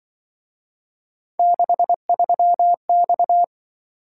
Morse code, heard as a keyed tone, about 23 wpm, 700 Hz. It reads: 63X
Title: Speed-Racing - Three Letter-Number Combinations - 20wpm